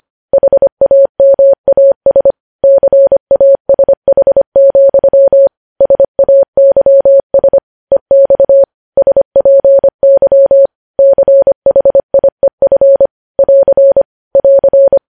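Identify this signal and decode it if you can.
Morse code, heard as a keyed tone, about 25 wpm, 550 Hz. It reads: HAMAS CASH, SAYS EX SPY CHIEF <AR> <AR>
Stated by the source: Morse code news headlines sent at 25 wpm.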